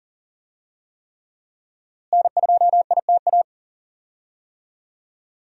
Morse code, heard as a keyed tone, about 40 wpm, 700 Hz. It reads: N2ITA
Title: USA Call Signs - 5 Characters - Repeated 3x - 40wpm